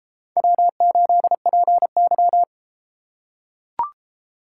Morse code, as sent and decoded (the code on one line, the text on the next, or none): .-- ---.. .--. -.--
W8PY